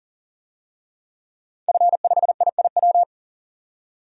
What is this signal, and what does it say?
Morse code, heard as a keyed tone, about 40 wpm, 700 Hz. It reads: F5IIW
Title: DXCC-100 Call Signs - 5 Characters - 40wpm